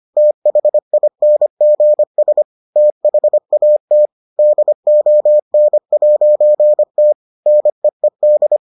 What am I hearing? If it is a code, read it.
Morse code, heard as a keyed tone, about 25 wpm, 600 Hz. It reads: THINGS THAT DON'T NEED